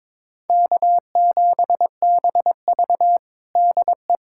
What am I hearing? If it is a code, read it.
Morse code, heard as a keyed tone, about 21 wpm, 700 Hz. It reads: K7BV DE
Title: Practice QSO - Exchange 4 - Ending - 22wpm